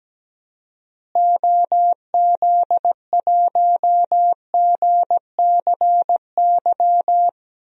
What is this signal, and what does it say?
Morse code, heard as a keyed tone, about 17 wpm, 700 Hz. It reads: OZ1GCY
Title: DXCC-100 Call Signs - 6 Characters - Repeated 3x - 17wpm